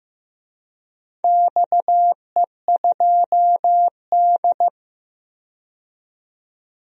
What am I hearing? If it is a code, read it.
Morse code, heard as a keyed tone, about 15 wpm, 700 Hz. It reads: XE2D
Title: DXCC-100 Call Signs - 4 Characters - 15wpm